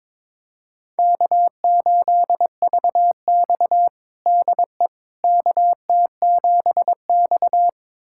Morse code, heard as a keyed tone, about 22 wpm, 700 Hz. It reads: K8VX DE KT7X
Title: USA Callsign DE Callsign - 4 Characters - 22wpm